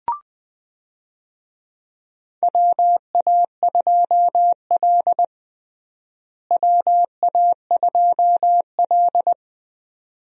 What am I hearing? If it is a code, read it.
Morse code, heard as a keyed tone, about 21 wpm, 700 Hz. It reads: WA2L WA2L
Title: USA Call Signs - 4 Characters - Repeated 2x - 20wpm